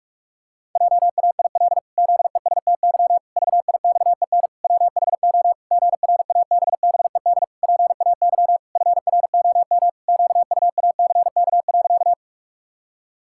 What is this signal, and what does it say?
Morse code, heard as a keyed tone, about 46 wpm, 700 Hz. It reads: JAIL ZESTY VIXEN WHO GRABBED PAY FROM QUACK.
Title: Pangram Challenge v2 - 45wpm